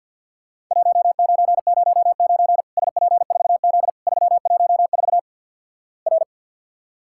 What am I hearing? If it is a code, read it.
Morse code, heard as a keyed tone, about 49 wpm, 700 Hz. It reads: 1909 SP4Z 314 R
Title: Worked All Europe DX Contest - QTC Exchange with Repeats - 50wpm